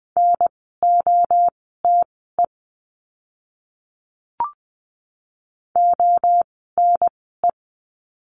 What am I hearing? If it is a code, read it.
Morse code, heard as a keyed tone, about 20 wpm, spaced as for 15 wpm, 700 Hz. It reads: NOTE ONE
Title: 054 Lesson 10f - Letter r - Words 20wpm